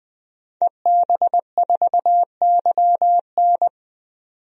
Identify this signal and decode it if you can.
Morse code, heard as a keyed tone, about 20 wpm, 700 Hz. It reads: EB4YN
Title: DXCC-100 Call Signs - 5 Characters - 20wpm